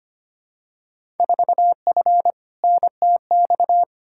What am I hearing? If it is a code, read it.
Morse code, heard as a keyed tone, about 25 wpm, 700 Hz. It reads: 4F NTX
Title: Field Day Exchanges Encore - 25wpm